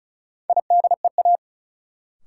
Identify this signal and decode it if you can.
Morse code, heard as a keyed tone, about 36 wpm, 700 Hz. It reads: IDEA
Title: Sentences from Top 300 Words - New Word Drill - 35wpm